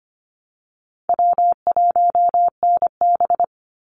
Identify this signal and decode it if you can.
Morse code, heard as a keyed tone, about 25 wpm, 700 Hz. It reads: W1NB
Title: USA Call Signs 25wpm